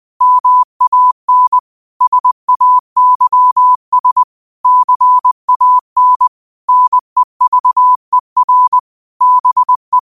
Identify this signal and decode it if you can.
Morse code, heard as a keyed tone, about 20 wpm, 1000 Hz. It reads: MAN SAYS CAN NEVER BE
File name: qotdfiles / 20wpm